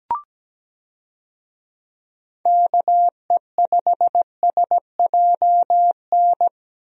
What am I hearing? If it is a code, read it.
Morse code, heard as a keyed tone, about 17 wpm, 700 Hz. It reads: KE5SJN
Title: USA Call Signs - 6 Characters - Repeated 3x - 17wpm